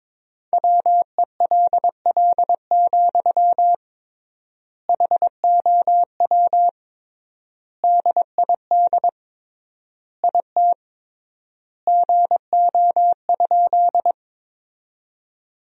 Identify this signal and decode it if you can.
Morse code, heard as a keyed tone, about 22 wpm, 700 Hz. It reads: WELL, HOW DID IT GO?
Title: 3x Word-Spacing - Sentences from Top 200 Words - 22wpm